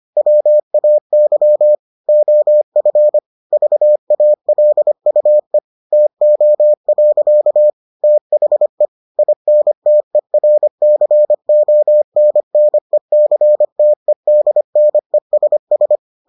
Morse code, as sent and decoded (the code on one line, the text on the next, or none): .-- .- -.-- / --- ..-. / ...- .- .-.. ..- . / - --- .-.-.- / - .... / .. -. - . .-. -.-. --- -. -. -.-. - . -.. -. ... ...
WAY OF VALUE TO. THE INTERCONNECTEDNESS